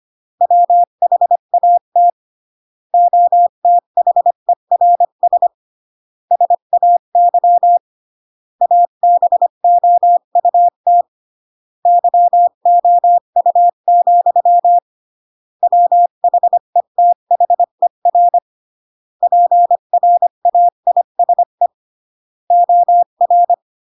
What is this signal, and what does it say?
Morse code, heard as a keyed tone, about 25 wpm, 700 Hz. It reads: WHAT OTHERS SAY ABOUT YOU, WHETHER PRAISE OR